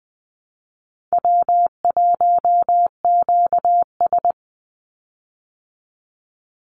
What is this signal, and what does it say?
Morse code, heard as a keyed tone, about 20 wpm, 700 Hz. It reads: W1QS
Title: Easy Call Signs Repeated 3x - 20wpm